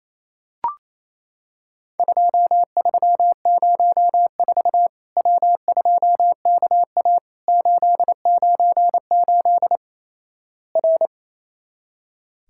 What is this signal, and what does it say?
Morse code, heard as a keyed tone, about 28 wpm, 700 Hz. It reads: 2304 W2KA 898 R
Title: Worked All Europe DX Contest - QTC Exchange - 28wpm